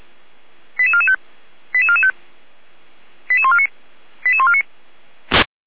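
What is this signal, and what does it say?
Five-tone selective call (ZVEI), heard as quick runs of five tones, tones of about 70 ms. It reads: 88445 88445 88149 88149
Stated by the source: ZVEI five-tone call 88265 received by SDR, from feuerwehr-orlamuende.org (Orlamünde fire brigade) audio archive.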